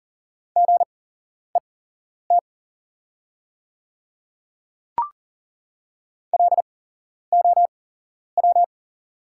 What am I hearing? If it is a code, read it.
Morse code, heard as a keyed tone, about 41 wpm, spaced as for 11 wpm, 700 Hz. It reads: GET LOW
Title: 8x Character Spacing - Top 200 Words - 40wpm